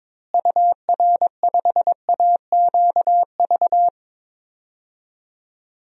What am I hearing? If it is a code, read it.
Morse code, heard as a keyed tone, about 22 wpm, 700 Hz. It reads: UR5AQV
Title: DXCC-100 Call Signs - 6 Characters - Repeated 2x - 22wpm